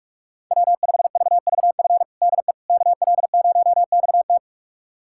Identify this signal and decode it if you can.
Morse code, heard as a keyed tone, about 46 wpm, 700 Hz. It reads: W5VVF DE KL0XT
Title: USA Callsign DE Callsign - 5 Characters - 45wpm